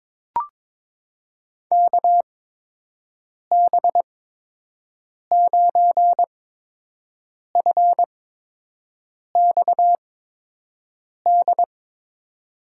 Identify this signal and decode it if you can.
Morse code, heard as a keyed tone, about 22 wpm, spaced as for 6 wpm, 700 Hz. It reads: KB9FXD
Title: USA Call Signs - 6 Characters - 8x Character Spacing - 22wpm